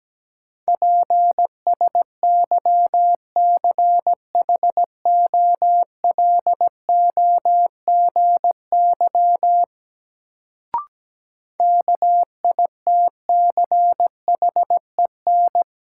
Morse code, heard as a keyed tone, about 17 wpm, 700 Hz. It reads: PSYCHOLOGY KITCHEN
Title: Sentences from Top 1400 Words - New Word Drill - 17wpm